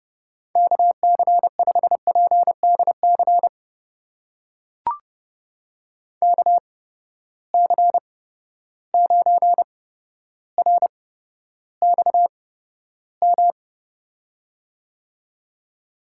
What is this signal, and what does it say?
Morse code, heard as a keyed tone, about 30 wpm, spaced as for 13 wpm, 700 Hz. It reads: KC5PDC KC9RXM
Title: USA Call Signs - 6 Characters - 8x Character Spacing - 30wpm